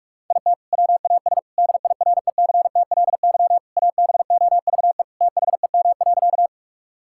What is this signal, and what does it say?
Morse code, heard as a keyed tone, about 46 wpm, 700 Hz. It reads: IT WAS DIRECTLY ABOVE THEM.
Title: Sentences from Top 1000 Words - Review Part 5 - 45wpm